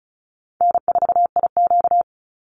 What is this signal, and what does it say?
Morse code, heard as a keyed tone, about 34 wpm, 700 Hz. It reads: N4IQ
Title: Call Signs - 35wpm